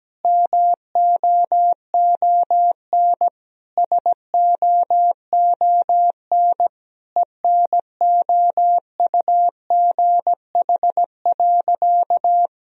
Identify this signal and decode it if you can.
Morse code, heard as a keyed tone, about 17 wpm, 700 Hz. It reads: MOON SOON ENOUGH.
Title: Sentences from Top 1000 Words - Review Part 3 - 17wpm